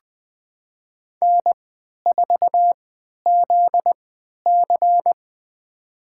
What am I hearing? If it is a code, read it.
Morse code, heard as a keyed tone, about 20 wpm, spaced as for 11 wpm, 700 Hz. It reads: N4ZC
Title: USA Call Signs - 4 Characters - 3x Character Spacing - 20wpm